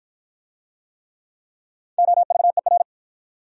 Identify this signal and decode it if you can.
Morse code, heard as a keyed tone, about 52 wpm, 700 Hz. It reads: OVER